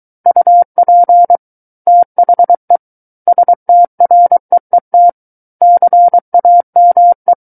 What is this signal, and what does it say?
Morse code, heard as a keyed tone, about 23 wpm, 700 Hz. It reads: UP THE STREET CAME
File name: Book One / 14-Chapter 14 - 23 WPM